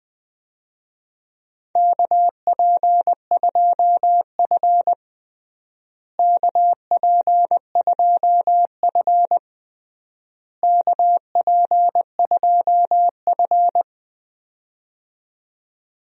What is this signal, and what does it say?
Morse code, heard as a keyed tone, about 20 wpm, 700 Hz. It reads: KP2F KP2F KP2F
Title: DXCC-100 Call Signs - 4 Characters - Repeated 3x - 20wpm